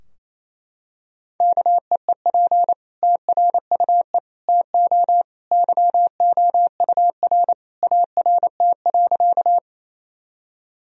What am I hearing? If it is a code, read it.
Morse code, heard as a keyed tone, about 28 wpm, 700 Hz. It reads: KEEP TRUE TO YOUR ART.
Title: Sentences from Top 1000 Words - Review Part 3 - 28wpm